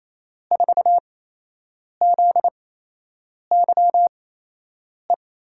Morse code, heard as a keyed tone, about 28 wpm, spaced as for 8 wpm, 700 Hz. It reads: 4ZYE
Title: USA Call Signs - 6 Characters - 8x Character Spacing - 28wpm